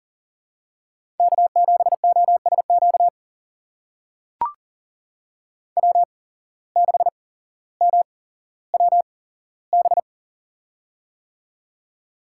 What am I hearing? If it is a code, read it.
Morse code, heard as a keyed tone, about 40 wpm, spaced as for 17 wpm, 700 Hz. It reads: K7OSQ W6MWB
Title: USA Call Signs - 5 Characters - 8x Character Spacing - 40wpm